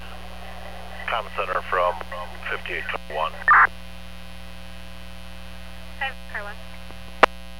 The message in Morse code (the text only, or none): none